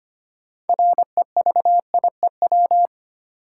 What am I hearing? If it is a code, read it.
Morse code, heard as a keyed tone, about 25 wpm, 700 Hz. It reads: REVIEW
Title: Sentences from Top 900 Words - New Word Drill - 25wpm